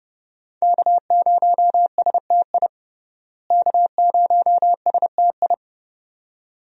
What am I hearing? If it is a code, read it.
Morse code, heard as a keyed tone, about 32 wpm, 700 Hz. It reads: K0STI K0STI
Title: USA Call Signs - 5 Characters - Repeated 2x - 30wpm